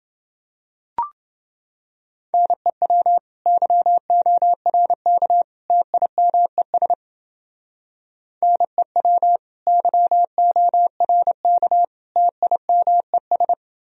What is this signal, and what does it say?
Morse code, heard as a keyed tone, about 30 wpm, 700 Hz. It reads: NEW YORK TIMES NEW YORK TIMES